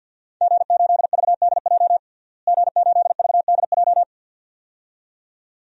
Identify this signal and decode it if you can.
Morse code, heard as a keyed tone, about 51 wpm, 700 Hz. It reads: G8VDJ G8VDJ
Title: DXCC-100 Call Signs - 5 Characters - Repeated 2x - 50wpm